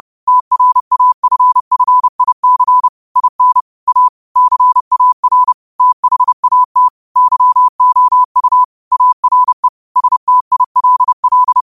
Morse code, heard as a keyed tone, about 30 wpm, 1000 Hz. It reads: TRAFFIC IN A CAR THAT YOU ARE STILL
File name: qotdfiles / 30wpm